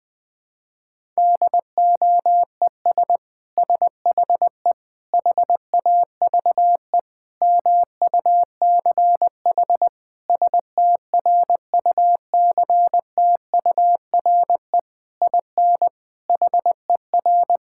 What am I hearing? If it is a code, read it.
Morse code, heard as a keyed tone, about 20 wpm, 700 Hz. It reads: DOES SHE HAVE MUCH STRUCTURE IN HER
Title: Sentences from Top 1000 Words - Review Part 6 - 20wpm